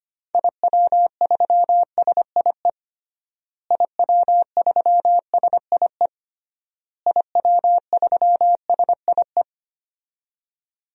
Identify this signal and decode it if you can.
Morse code, heard as a keyed tone, about 25 wpm, 700 Hz. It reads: IW3SIE IW3SIE IW3SIE
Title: DXCC-100 Call Signs - 6 Characters - Repeated 3x - 25wpm